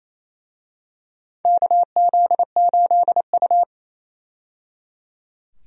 Morse code, heard as a keyed tone, about 28 wpm, 700 Hz. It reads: KZ8U
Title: USA Call Signs - 4 Characters - 28wpm